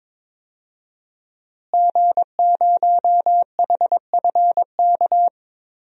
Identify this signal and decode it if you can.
Morse code, heard as a keyed tone, about 22 wpm, 700 Hz. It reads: G0HFK
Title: DXCC-100 Call Signs - 5 Characters - Repeated 3x - 22wpm